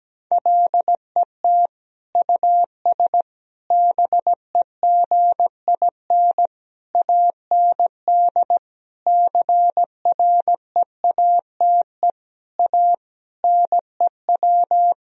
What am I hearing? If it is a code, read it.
Morse code, heard as a keyed tone, about 17 wpm, 700 Hz. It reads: LET US BEGIN AND CREATE A NEW
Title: Sentences from Top 1000 Words - Review Part 3 - 17wpm